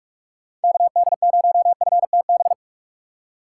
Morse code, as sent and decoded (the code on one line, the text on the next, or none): -.- -.. ----- ..-. - -...
KD0FTB